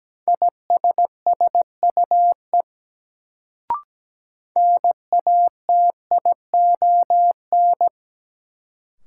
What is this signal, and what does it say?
Morse code, heard as a keyed tone, about 17 wpm, 700 Hz. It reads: ISSUE NATION
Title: Sentences from Top 900 Words - New Word Drill - 17wpm